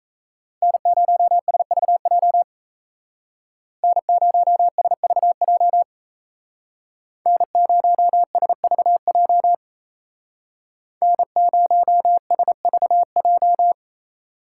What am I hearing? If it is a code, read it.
Morse code, heard as a keyed tone, about 36 wpm, 700 Hz. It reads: N0SVJ N0SVJ N0SVJ N0SVJ